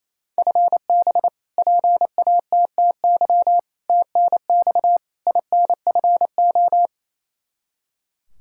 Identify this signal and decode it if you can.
Morse code, heard as a keyed tone, about 28 wpm, 700 Hz. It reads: FB PATTY TNX INFO
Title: K1USN SST QSO Practice 28wpm